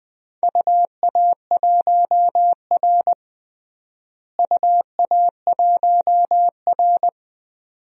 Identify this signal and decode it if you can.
Morse code, heard as a keyed tone, about 20 wpm, 700 Hz. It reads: UA1R UA1R
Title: DXCC-100 Call Signs - 4 Characters - Repeated 2x - 20wpm